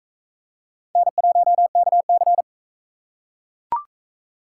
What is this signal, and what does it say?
Morse code, heard as a keyed tone, about 40 wpm, 700 Hz. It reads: N1KC